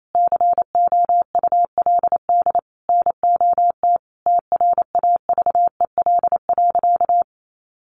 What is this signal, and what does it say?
Morse code, heard as a keyed tone, about 28 wpm, 700 Hz. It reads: COULD NOT TRAVEL.
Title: Sentences from Top 1300 Words 28wpm